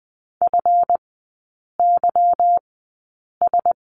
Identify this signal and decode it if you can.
Morse code, heard as a keyed tone, about 20 wpm, spaced as for 8 wpm, 690 Hz. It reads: FYS
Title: 159 Lesson 26e - Letter g - Three Characters 20wpm